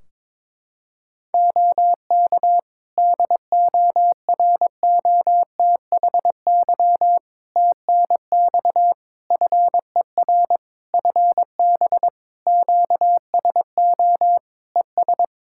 Morse code, heard as a keyed tone, about 22 wpm, 700 Hz. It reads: OK DOROTHY TNX FER FB QSO ES
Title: Practice QSO - Exchange 4 - Ending - 22wpm